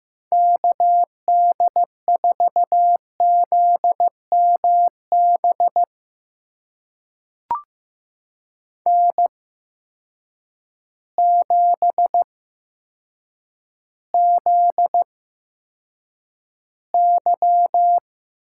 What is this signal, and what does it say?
Morse code, heard as a keyed tone, about 16 wpm, spaced as for 8 wpm, 700 Hz. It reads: KD4ZMB N7ZY
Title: USA Call Signs - 8x Character Spacing - 15wpm